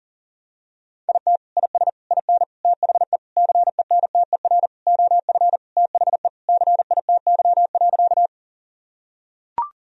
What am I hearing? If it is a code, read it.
Morse code, heard as a keyed tone, about 40 wpm, 700 Hz. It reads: IT IS IN THE CENTER OF THE CITY.
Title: Sentences from Top 1000 Words - Review Part 2 - 40wpm